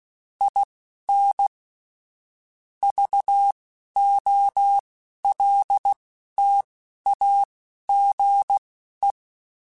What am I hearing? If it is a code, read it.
Morse code, heard as a keyed tone, about 16 wpm, spaced as for 11 wpm, 800 Hz. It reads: IN VOLTAGE
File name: Technical Subjects / Technical Subjects-mp3s / XC @10